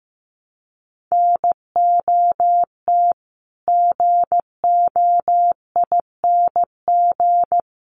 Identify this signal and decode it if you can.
Morse code, heard as a keyed tone, about 15 wpm, 700 Hz. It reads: NOT GOING
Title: Sets of 4 Words 15wpm